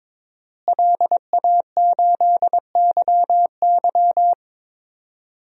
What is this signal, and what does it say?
Morse code, heard as a keyed tone, about 22 wpm, 700 Hz. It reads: LA8YY
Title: DXCC-100 Call Signs - 5 Characters - Repeated 3x - 22wpm